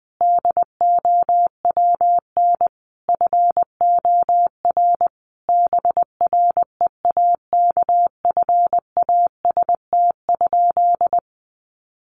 Morse code, heard as a keyed tone, about 20 wpm, 700 Hz. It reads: DOWN FOR BREAKFAST?
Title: Sentences from Top 1500 Words 20wpm